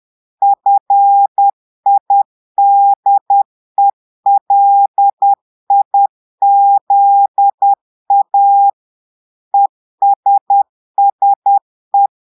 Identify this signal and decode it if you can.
Morse code, heard as a keyed tone, about 10 wpm, 800 Hz. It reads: FIDELIZA ESSE
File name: portugais texte en clair 2 vitesse 10 mots minutes